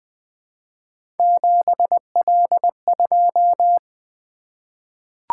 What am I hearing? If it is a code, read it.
Morse code, heard as a keyed tone, about 20 wpm, 700 Hz. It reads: 7L2
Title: Three Letter-Number Combinations - 20wpm